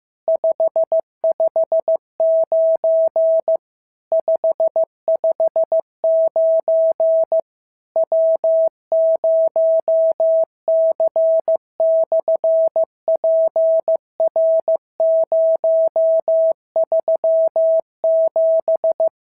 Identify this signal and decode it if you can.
Morse code, heard as a keyed tone, about 15 wpm, 650 Hz. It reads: 559 559 W0C/PR037
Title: SOTA QSO Practice 15wpm